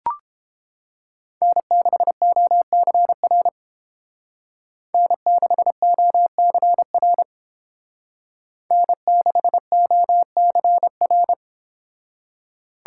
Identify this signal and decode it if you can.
Morse code, heard as a keyed tone, about 31 wpm, 700 Hz. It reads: N6OCR N6OCR N6OCR